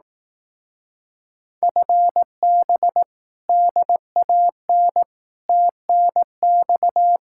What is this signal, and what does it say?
Morse code, heard as a keyed tone, about 18 wpm, 700 Hz. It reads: FB DAN TNX
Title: Speed-Racing Common QSO Phrases Encore - 15wpm